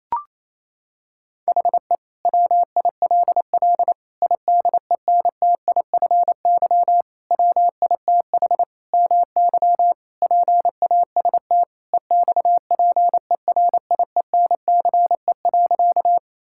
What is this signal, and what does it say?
Morse code, heard as a keyed tone, about 28 wpm, 700 Hz. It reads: HE WILL IDENTIFY WITH MY PAST EXPERIENCE.
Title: Sentences from Top 1000 Words - Review Part 6 - 28wpm